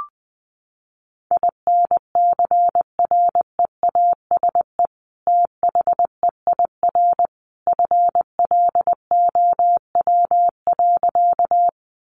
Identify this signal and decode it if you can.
Morse code, heard as a keyed tone, about 20 wpm, 700 Hz. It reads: INCREASE THEIR FLOW.
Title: Sentences from Top 1200 Words 20wpm